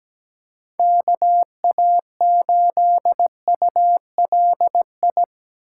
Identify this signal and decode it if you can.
Morse code, heard as a keyed tone, about 20 wpm, 700 Hz. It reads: KA8ULI
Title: Speed Racing - USA 2x3 Call Signs 17wpm